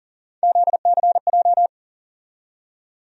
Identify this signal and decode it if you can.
Morse code, heard as a keyed tone, about 40 wpm, 700 Hz. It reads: ZCJ
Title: USA Call Sign Postfixes - 3 Letters 40wpm